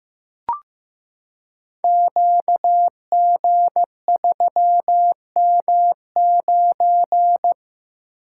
Morse code, heard as a keyed tone, about 15 wpm, 700 Hz. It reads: QG3M9
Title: Code Groups - 5 Letter:Numbers - 15wpm